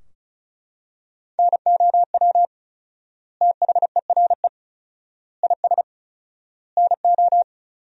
Morse code, heard as a keyed tone, about 36 wpm, 700 Hz. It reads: NOW THERE IS NO
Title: 4x Word-Spacing - Sentences from Top 100 Words - 35wpm